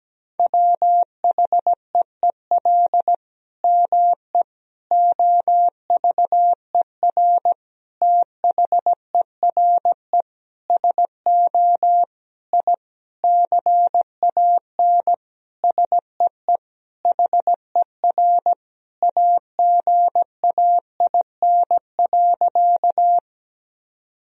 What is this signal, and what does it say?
Morse code, heard as a keyed tone, about 17 wpm, 700 Hz. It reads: WHEEL ME OVER THERE SO I CAN SEE HER AGAIN.
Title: Sentences from Top 1000 Words - Review Part 3 - 17wpm